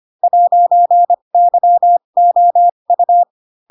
Morse code, heard as a keyed tone, about 25 wpm, spaced as for 22 wpm, 700 Hz. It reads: 'YOU
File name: Pinocchio / Pinocchio ch7 18 25 ik1hge